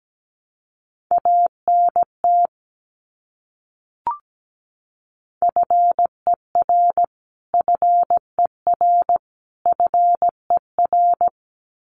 Morse code, heard as a keyed tone, about 17 wpm, 700 Hz. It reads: ANT FER FER FER
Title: Abbreviations Repeated 3x - 17wpm